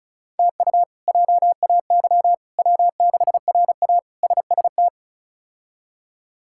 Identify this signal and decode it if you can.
Morse code, heard as a keyed tone, about 34 wpm, 700 Hz. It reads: TU JAY W6RA SST
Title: K1USN SST QSO Practice - 35wpm